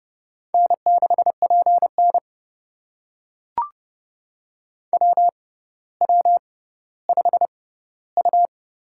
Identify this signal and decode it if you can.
Morse code, heard as a keyed tone, about 30 wpm, spaced as for 15 wpm, 700 Hz. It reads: N6PN WW5U
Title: USA Call Signs - 4 Characters - 6x Character Spacing - 30wpm